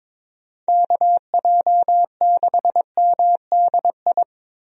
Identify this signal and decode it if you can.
Morse code, heard as a keyed tone, about 23 wpm, 700 Hz. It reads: KJ6MDI
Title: USA Call Signs - 6 Characters - 22wpm